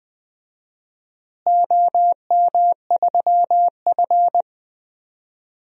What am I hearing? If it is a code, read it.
Morse code, heard as a keyed tone, about 21 wpm, 700 Hz. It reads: OM3F
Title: DXCC-100 Call Signs - 4 Characters - Repeated 2x - 20wpm